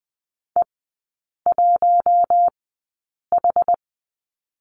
Morse code, heard as a keyed tone, about 20 wpm, spaced as for 8 wpm, 700 Hz. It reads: E1H
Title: 121 Lesson 20e - Number 3 - Three Characters 20wpm